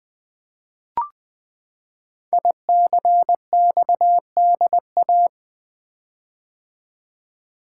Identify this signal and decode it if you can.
Morse code, heard as a keyed tone, about 20 wpm, 700 Hz. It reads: ICXDA
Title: Code Groups - 5 Letter:Numbers - 20wpm